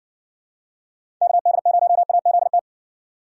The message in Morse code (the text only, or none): KD0AZT